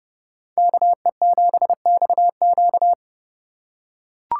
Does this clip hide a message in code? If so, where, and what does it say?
Yes: Morse code from 0.6 to 2.9 s, heard as a keyed tone, about 27 wpm, 700 Hz. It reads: KE7XQ